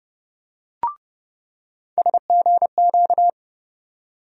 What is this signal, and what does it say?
Morse code, heard as a keyed tone, about 30 wpm, 700 Hz. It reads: SGQ
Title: USA Call Sign Postfixes - 3 Letters 30wpm